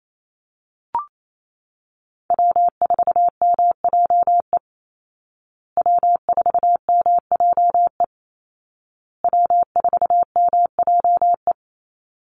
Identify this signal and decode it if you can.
Morse code, heard as a keyed tone, about 28 wpm, 700 Hz. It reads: W4MJE W4MJE W4MJE
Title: Call Signs Repeated 2x - 28wpm